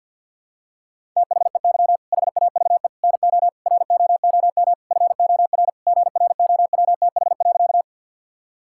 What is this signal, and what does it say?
Morse code, heard as a keyed tone, about 51 wpm, 700 Hz. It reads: THEY HAVE NO ROOM FOR GROWTH.
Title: Sentences from Top 1000 Words - Review Part 5 - 50wpm